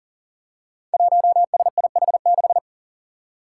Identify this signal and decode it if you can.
Morse code, heard as a keyed tone, about 37 wpm, 700 Hz. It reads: 1SIH6
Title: Five Challenging Letter-Number Combinations 40wpm